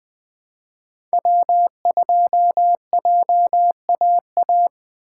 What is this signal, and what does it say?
Morse code, heard as a keyed tone, about 20 wpm, 700 Hz. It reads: W2JAA
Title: USA Call Signs - 5 Characters - 2x Character Spacing - 20wpm